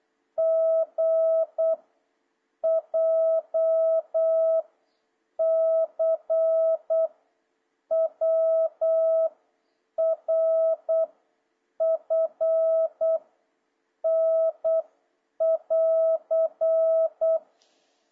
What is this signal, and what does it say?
Morse code, heard as a keyed tone, about 8 wpm, 650 Hz. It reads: G J C W R F N <AR>
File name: jh8cbh / 2026cw